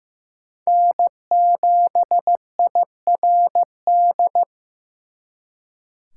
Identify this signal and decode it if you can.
Morse code, heard as a keyed tone, about 18 wpm, 700 Hz. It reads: N7IRD
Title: Speed Racing - USA 1x3 Call Signs 15wpm